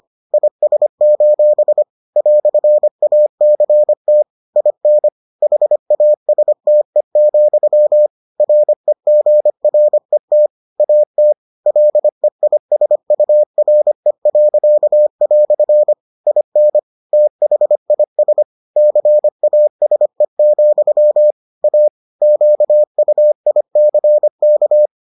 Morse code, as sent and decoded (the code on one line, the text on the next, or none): .. ... ---... / .-..-. .- -.-. - / .. -. / .... .- ... - . --..-- / .-. --. .-. - / .- - / .-.. .. ... ..- .-. .-.-.- .-..-. / .. -. / - .... .. ... / -.-. .- ... --..-- / .- / --.- ..- .. -.-. -.-
IS: "ACT IN HASTE, REGRET AT LEISURE." IN THIS CASE, A QUICK